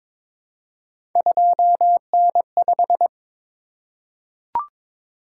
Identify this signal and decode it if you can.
Morse code, heard as a keyed tone, about 23 wpm, 700 Hz. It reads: 2N5
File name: practice / 22wpm / Three Letter-Number Combinations - 22wpm